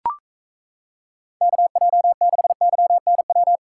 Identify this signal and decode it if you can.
Morse code, heard as a keyed tone, about 36 wpm, 700 Hz. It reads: KJ6YNW